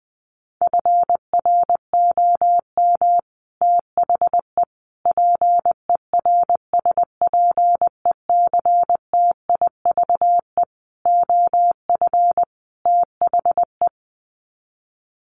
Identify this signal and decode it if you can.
Morse code, heard as a keyed tone, about 20 wpm, 700 Hz. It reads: FROM THE PERSPECTIVE OF THE
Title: Sets of 5 Words Encore2 20wpm